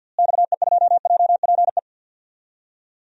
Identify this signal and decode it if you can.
Morse code, heard as a keyed tone, about 52 wpm, 700 Hz. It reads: XE2JPE
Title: DXCC-100 Call Signs - 6 Characters - Repeated 2x - 50wpm